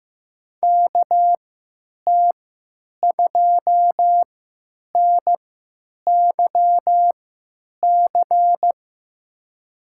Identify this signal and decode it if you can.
Morse code, heard as a keyed tone, about 15 wpm, spaced as for 8 wpm, 700 Hz. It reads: KT2NYC
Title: USA Call Signs - 3x Character Spacing - 15wpm